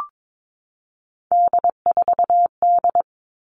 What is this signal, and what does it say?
Morse code, heard as a keyed tone, about 22 wpm, 700 Hz. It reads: D4D
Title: All Cut Numbers - 3-Digit 22wpm